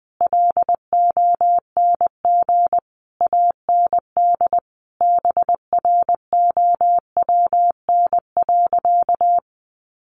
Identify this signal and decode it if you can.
Morse code, heard as a keyed tone, about 20 wpm, 700 Hz. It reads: LONG AND BROWN.
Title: Sentences from Top 1200 Words 20wpm